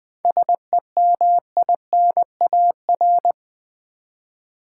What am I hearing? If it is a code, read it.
Morse code, heard as a keyed tone, about 20 wpm, 700 Hz. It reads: SEMINAR
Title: 7 Letter Words - First 3 Letters Given - 20wpm